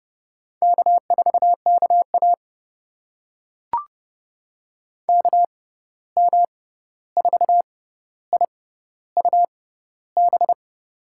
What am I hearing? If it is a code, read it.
Morse code, heard as a keyed tone, about 30 wpm, spaced as for 14 wpm, 700 Hz. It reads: K4KA KM4IUB
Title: USA Call Signs - 6x Character Spacing - 30wpm